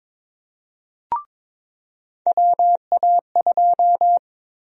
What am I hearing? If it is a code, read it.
Morse code, heard as a keyed tone, about 22 wpm, 700 Hz. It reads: WA2
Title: USA Call Sign Prefixes - 22wpm